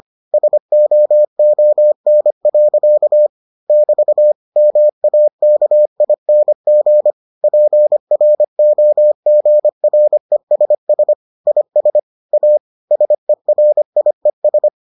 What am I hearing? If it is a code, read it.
Morse code, heard as a keyed tone, about 25 wpm, 600 Hz. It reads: SOON. <BT> MAKING PROGRESS IS A SERIES